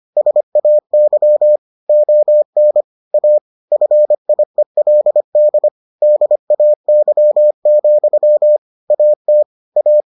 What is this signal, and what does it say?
Morse code, heard as a keyed tone, about 25 wpm, 600 Hz. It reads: SAY ON A FIELD DAY, AT A